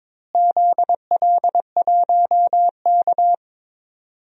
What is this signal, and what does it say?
Morse code, heard as a keyed tone, about 22 wpm, 700 Hz. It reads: ZL1K
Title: DXCC-100 Call Signs - 4 Characters - Repeated 2x - 22wpm